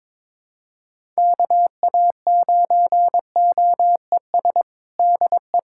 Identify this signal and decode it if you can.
Morse code, heard as a keyed tone, about 22 wpm, 700 Hz. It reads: KA9OES DE
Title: USA Callsign DE Callsign 22wpm